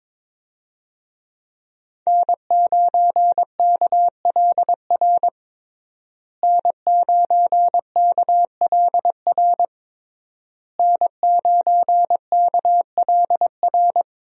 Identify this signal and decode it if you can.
Morse code, heard as a keyed tone, about 22 wpm, 700 Hz. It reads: N9KLR N9KLR N9KLR
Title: USA Call Signs - 5 Characters - Repeated 3x - 22wpm